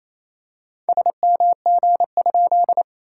Morse code, heard as a keyed tone, about 27 wpm, 700 Hz. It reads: SMG?
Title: SOTA QSO Practice 28wpm